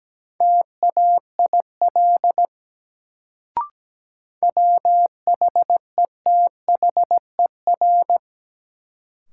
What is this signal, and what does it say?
Morse code, heard as a keyed tone, about 17 wpm, 700 Hz. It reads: TAIL WHETHER
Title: Sentences from Top 1000 Words - Review Part 3 - New Word Drill - 17wpm